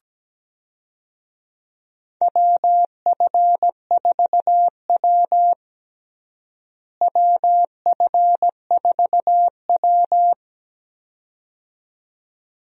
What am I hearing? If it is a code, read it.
Morse code, heard as a keyed tone, about 17 wpm, 700 Hz. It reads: WF4W WF4W
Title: USA Call Signs - 4 Characters - Repeated 2x - 17wpm